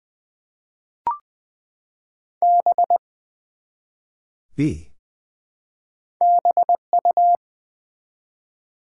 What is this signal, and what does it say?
Morse code, heard as a keyed tone, about 20 wpm, 700 Hz. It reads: B BU
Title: Build A Word - 1 Letter At A Time - 3-Letter Words - 20wpm